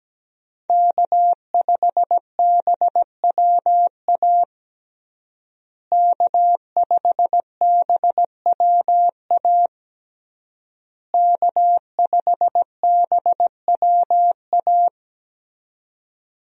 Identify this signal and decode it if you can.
Morse code, heard as a keyed tone, about 17 wpm, 700 Hz. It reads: K5BWA K5BWA K5BWA
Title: USA Call Signs - 5 Characters - Repeated 3x - 17wpm